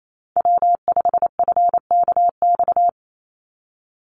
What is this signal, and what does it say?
Morse code, heard as a keyed tone, about 28 wpm, 700 Hz. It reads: W5FKX
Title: Call Signs - 28wpm